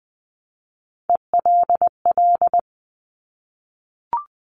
Morse code, heard as a keyed tone, about 20 wpm, 700 Hz. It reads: ELL
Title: Top Three Letter Combinations in Words 20wpm